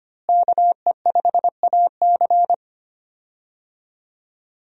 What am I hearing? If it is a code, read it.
Morse code, heard as a keyed tone, about 25 wpm, 700 Hz. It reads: KE5AC
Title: USA Call Signs - 5 Characters - Repeated 2x - 25wpm